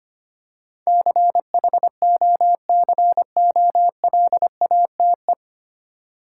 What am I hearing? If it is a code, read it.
Morse code, heard as a keyed tone, about 25 wpm, 700 Hz. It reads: CHOCOLATE
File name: Nine Letter Words - 25wpm